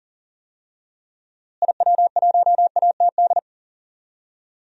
Morse code, heard as a keyed tone, about 40 wpm, 700 Hz. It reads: IW1ATD
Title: DXCC-100 Call Signs - 6 Characters - Repeated 3x - 40wpm